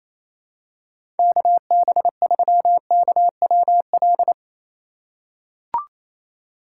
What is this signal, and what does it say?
Morse code, heard as a keyed tone, about 28 wpm, 700 Hz. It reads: KB3KWL
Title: USA Call Signs - 6 Characters - 6x Character Spacing - 28wpm